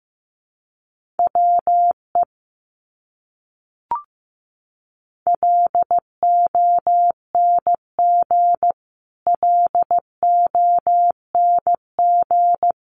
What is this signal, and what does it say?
Morse code, heard as a keyed tone, about 15 wpm, 700 Hz. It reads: WE LONG LONG
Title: Top 200 Words Repeated 15wpm